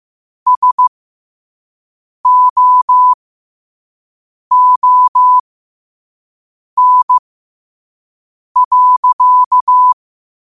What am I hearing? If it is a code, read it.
Morse code, heard as a keyed tone, about 15 wpm, spaced as for 5 wpm, 1000 Hz. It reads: SOON.
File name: qotdfiles / 5wpm